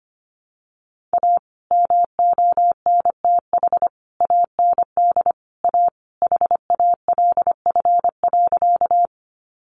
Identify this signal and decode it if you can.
Morse code, heard as a keyed tone, about 25 wpm, 700 Hz. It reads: A MONTH AND A HALF.
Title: Sentences from Top 800 Words 25wpm